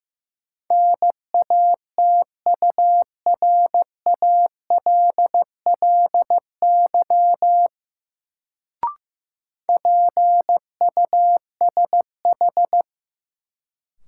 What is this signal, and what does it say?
Morse code, heard as a keyed tone, about 15 wpm, 700 Hz. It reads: NATURALLY PUSH
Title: Sentences from Top 1300 Words - New Word Drill - 15wpm